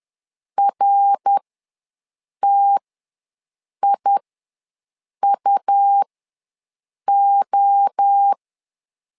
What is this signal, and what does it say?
Morse code, heard as a keyed tone, about 10 wpm, spaced as for 6 wpm, 800 Hz. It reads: RTIUO
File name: LEZIONI / 3 MWRO / 3 MWRO ws 10wpm 2310